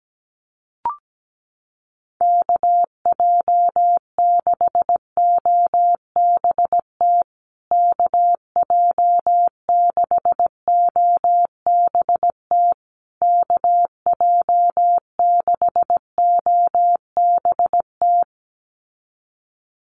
Morse code, heard as a keyed tone, about 17 wpm, 700 Hz. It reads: KJ6OBT KJ6OBT KJ6OBT
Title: USA Call Signs Repeated 3x 17wpm